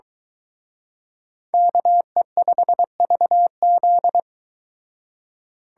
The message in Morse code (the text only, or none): KE5VZ